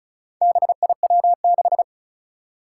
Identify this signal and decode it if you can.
Morse code, heard as a keyed tone, about 36 wpm, 700 Hz. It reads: BIW6
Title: Four Challenging Letter-Number Combinations 35wpm